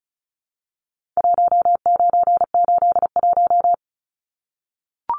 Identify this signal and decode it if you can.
Morse code, heard as a keyed tone, about 34 wpm, 700 Hz. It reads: 1981
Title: Numbers - Contest Style - 4-Digit 35wpm